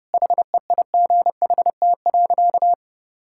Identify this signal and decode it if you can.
Morse code, heard as a keyed tone, about 32 wpm, 700 Hz. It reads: HEIGHT.
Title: Sentences from Top 1600 Words 30wpm